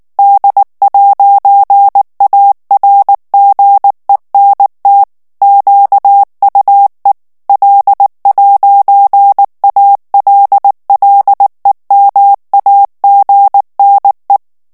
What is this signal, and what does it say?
Morse code, heard as a keyed tone, about 19 wpm, 800 Hz. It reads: D'ARGENT QUE L'ALLEMAGNE
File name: morse mp3 / FRANCAIS TEXTE CLAIR 1 VITESSE 18 MOTS MINUTES